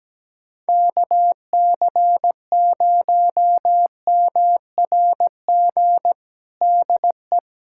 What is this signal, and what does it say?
Morse code, heard as a keyed tone, about 17 wpm, 700 Hz. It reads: KC0MRG DE
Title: USA Callsign DE Callsign - 6 Characters - 17wpm